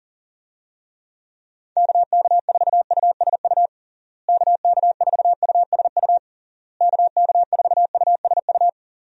Morse code, heard as a keyed tone, about 40 wpm, 700 Hz. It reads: KK4USU KK4USU KK4USU
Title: USA Call Signs - 6 Characters - Repeated 3x - 40wpm